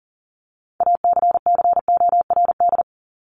Morse code, heard as a keyed tone, about 40 wpm, 700 Hz. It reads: ACCORD
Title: ING Suffix - Top 100 Words with Base Words 40wpm